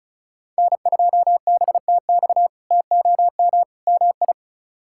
Morse code, heard as a keyed tone, about 36 wpm, 700 Hz. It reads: N2BTX TOM MI
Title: K1USN SST QSO Practice - 35wpm